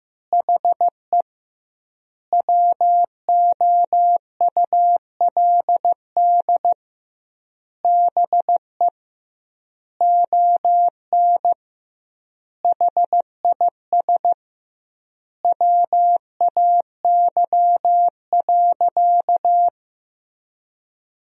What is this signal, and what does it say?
Morse code, heard as a keyed tone, about 15 wpm, 700 Hz. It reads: HE WOULD BE ON HIS WAY.
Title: 2x Word-Spacing - Sentences from Top 300 Words - 15wpm